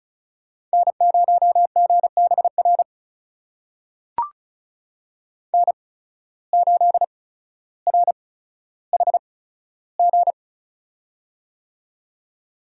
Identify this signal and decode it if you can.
Morse code, heard as a keyed tone, about 36 wpm, spaced as for 15 wpm, 700 Hz. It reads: N0GBR N8RHG
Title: USA Call Signs - 5 Characters - 8x Character Spacing - 35wpm